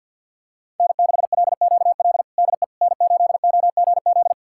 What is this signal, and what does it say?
Morse code, heard as a keyed tone, about 49 wpm, 700 Hz. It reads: N6LQL DE N8OGZ
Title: USA Callsign DE Callsign - 5 Characters - 50wpm